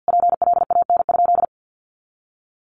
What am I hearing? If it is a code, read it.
Morse code, heard as a keyed tone, about 50 wpm, 700 Hz. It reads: PLAN?
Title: Sentences from Top 500 Words 50wpm